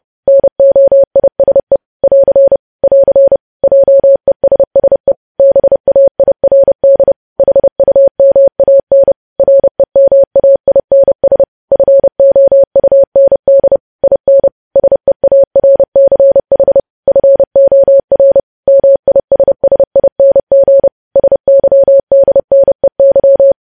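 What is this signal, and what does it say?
Morse code, heard as a keyed tone, about 30 wpm, 550 Hz. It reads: NOISE <AR> <AR> JESSE BAIRD HUMAN REMAINS FOUND IN SEARCH FOR MISSING SYDNEY
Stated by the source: News headlines sent in Morse at 30 WPM.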